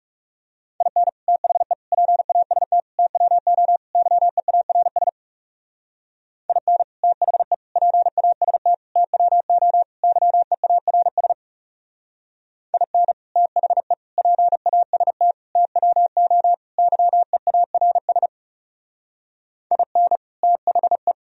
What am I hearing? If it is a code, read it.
Morse code, heard as a keyed tone, about 38 wpm, 700 Hz. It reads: IN THE PAST TWO YEARS IN THE PAST TWO YEARS IN THE PAST TWO YEARS IN THE